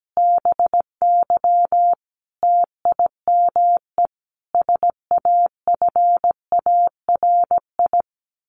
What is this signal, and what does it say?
Morse code, heard as a keyed tone, about 17 wpm, 700 Hz. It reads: BY TIME SAFARI
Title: A Sound Of Thunder 17wpm